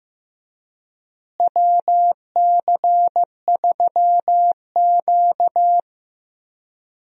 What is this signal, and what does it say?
Morse code, heard as a keyed tone, about 15 wpm, 700 Hz. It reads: WC3Q
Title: USA Call Signs - 4 Characters - Repeated 2x - 15wpm